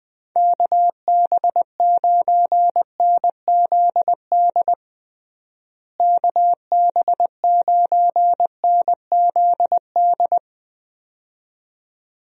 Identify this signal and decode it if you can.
Morse code, heard as a keyed tone, about 20 wpm, 700 Hz. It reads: KB9NZD KB9NZD
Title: USA Call Signs - 6 Characters - Repeated 2x - 20wpm